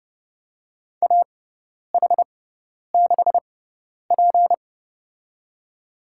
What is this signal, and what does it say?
Morse code, heard as a keyed tone, about 31 wpm, spaced as for 10 wpm, 700 Hz. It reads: AH6P
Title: USA Call Signs - 4 Characters - 6x Character Spacing - 30wpm